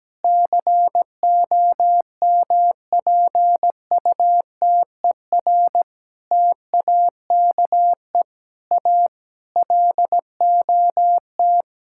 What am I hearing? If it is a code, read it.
Morse code, heard as a keyed tone, about 17 wpm, 700 Hz. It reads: COMPUTER TAKE A LOT
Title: Sentences from Top 1000 Words - Review Part 6 - 17wpm